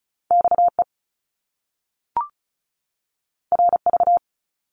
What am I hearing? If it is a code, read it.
Morse code, heard as a keyed tone, about 36 wpm, 700 Hz. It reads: XE RV
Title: Top Two Letter Combinations in Words 35wpm